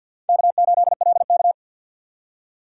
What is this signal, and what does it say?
Morse code, heard as a keyed tone, about 61 wpm, 700 Hz. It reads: K8LK
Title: Speed Racing - USA Call Signs 50wpm